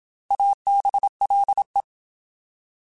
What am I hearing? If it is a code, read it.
Morse code, heard as a keyed tone, about 27 wpm, 800 Hz. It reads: ABLE
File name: Alphabetical / Alphabetical mp3s / As @25